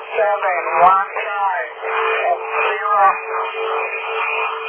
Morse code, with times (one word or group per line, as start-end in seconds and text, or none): none